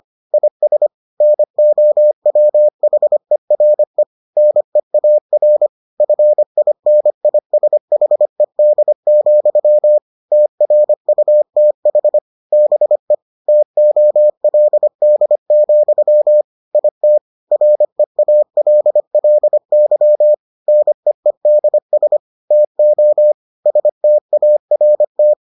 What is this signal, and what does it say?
Morse code, heard as a keyed tone, about 25 wpm, 600 Hz. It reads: IS NOWHERE NEAR FINISHED, TRUTH BE TOLD, IT REALLY NEEDS TO START